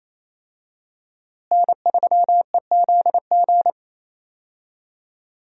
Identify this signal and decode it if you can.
Morse code, heard as a keyed tone, about 28 wpm, 700 Hz. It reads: N3EZG
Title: USA Call Signs - 5 Characters - Repeated 3x - 28wpm